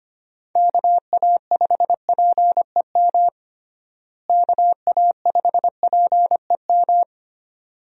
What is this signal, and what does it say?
Morse code, heard as a keyed tone, about 25 wpm, 700 Hz. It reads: KA5PEM KA5PEM
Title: USA Call Signs - 6 Characters - Repeated 2x - 25wpm